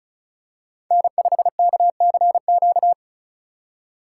Morse code, heard as a keyed tone, about 36 wpm, 690 Hz. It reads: N5KCQ
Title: USA Call Signs - 3x Character Spacing - 35wpm